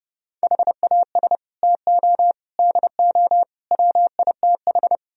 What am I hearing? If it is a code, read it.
Morse code, heard as a keyed tone, about 35 wpm, 700 Hz. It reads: HAS TO DO WITH
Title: Speed-Racing - Sets of 4 Words 30wpm